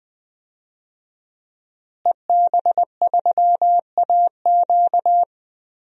Morse code, heard as a keyed tone, about 20 wpm, 700 Hz. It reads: EB3AQ
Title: DXCC-100 Call Signs - 5 Characters - Repeated 2x - 20wpm